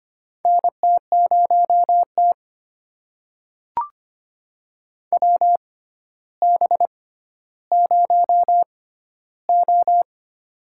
Morse code, heard as a keyed tone, about 25 wpm, spaced as for 13 wpm, 700 Hz. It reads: NT0T WB0O
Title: USA Call Signs - 4 Characters - 6x Character Spacing - 25wpm